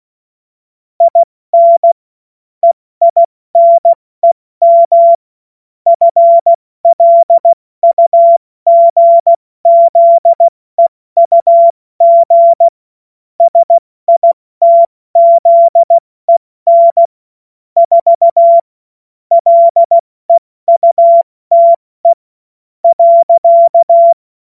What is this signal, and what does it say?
Morse code, heard as a keyed tone, about 16 wpm, spaced as for 14 wpm, 680 Hz. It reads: IN EINEM FLUGZEUG SITZEN 4 LEUTE.
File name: Morsekurs / Witze / Witz 16. Februar 2021, 70 F80